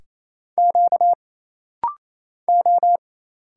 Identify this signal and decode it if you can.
Morse code, heard as a keyed tone, about 27 wpm, 700 Hz. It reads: Q O
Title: Single Letter-Number Alt - Rapid-Fire 28wpm